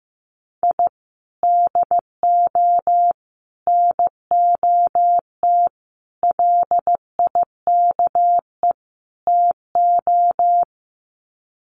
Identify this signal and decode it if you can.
Morse code, heard as a keyed tone, about 15 wpm, 700 Hz. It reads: I DO NOT LIKE TO
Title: Sets of 5 Words 15wpm